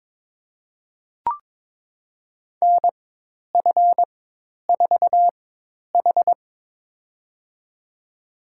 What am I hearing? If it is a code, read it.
Morse code, heard as a keyed tone, about 22 wpm, spaced as for 10 wpm, 700 Hz. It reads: NF4H
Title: USA Call Signs - 4x Character Spacing - 22wpm